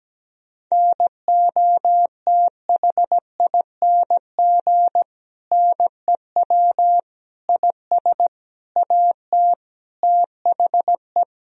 Morse code, heard as a keyed tone, about 17 wpm, 700 Hz. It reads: NOTHING NEW IS AT THE
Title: Sentences from Top 1000 Words - Review Part 1 - 17wpm